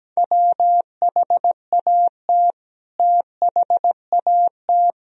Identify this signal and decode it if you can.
Morse code, heard as a keyed tone, about 17 wpm, 700 Hz. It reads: WHAT THAT
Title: Sentences from Top 1000 Words - Review Part 2 - 17wpm